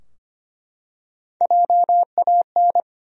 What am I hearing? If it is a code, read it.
Morse code, heard as a keyed tone, about 25 wpm, 700 Hz. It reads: JAN
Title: USA Common Names - 25wpm